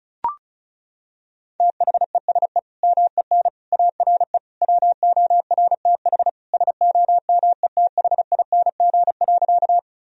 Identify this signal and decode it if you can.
Morse code, heard as a keyed tone, about 36 wpm, 700 Hz. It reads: THESE MEN ARE WORTH SOMETHING.
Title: Sentences from Top 1000 Words - Review Part 1 - 35wpm